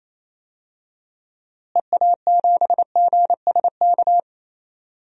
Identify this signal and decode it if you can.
Morse code, heard as a keyed tone, about 28 wpm, 700 Hz. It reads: EA7GSK
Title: DXCC-100 Call Signs - 6 Characters - Repeated 3x - 28wpm